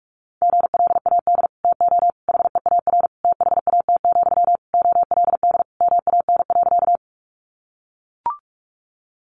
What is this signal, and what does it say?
Morse code, heard as a keyed tone, about 44 wpm, 700 Hz. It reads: GLAD TO HEAR THAT, OLD MAN.
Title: Sentences from Top 800 Words 45wpm